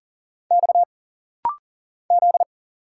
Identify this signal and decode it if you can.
Morse code, heard as a keyed tone, about 40 wpm, 700 Hz. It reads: X Z
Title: Single Letters - Rapid-Fire 40wpm